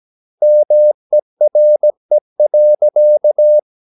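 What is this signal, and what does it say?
Morse code, heard as a keyed tone, about 17 wpm, 600 Hz. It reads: MERE.
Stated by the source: Slow (QRS) Morse practice at 17 wpm, from CW med Gnister.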